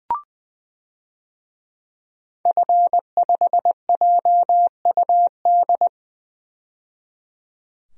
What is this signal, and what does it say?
Morse code, heard as a keyed tone, about 21 wpm, 700 Hz. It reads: F5JUD
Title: DXCC-100 Call Signs - 5 Characters - 20wpm